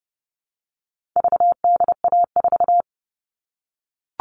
Morse code, heard as a keyed tone, about 30 wpm, 700 Hz. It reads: VDA4
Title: All Cut Numbers - 4-Digit 30wpm